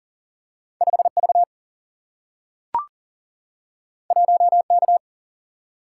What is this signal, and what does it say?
Morse code, heard as a keyed tone, about 39 wpm, 700 Hz. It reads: 5V 1K
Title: Single-Letter Single-Number 40wpm